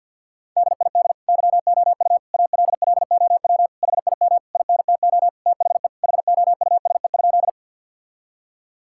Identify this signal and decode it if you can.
Morse code, heard as a keyed tone, about 50 wpm, 700 Hz. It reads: DID YOU ALLOW HIM INTO THE HOUSE?
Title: Sentences from Top 1000 Words - Review Part 5 - 50wpm